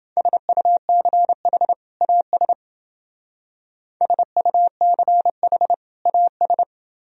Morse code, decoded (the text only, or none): SUCH AS SUCH AS